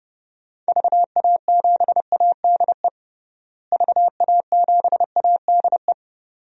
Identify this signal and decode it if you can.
Morse code, heard as a keyed tone, about 30 wpm, 700 Hz. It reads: VA7ADE VA7ADE
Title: DXCC-100 Call Signs - 6 Characters - Repeated 2x - 30wpm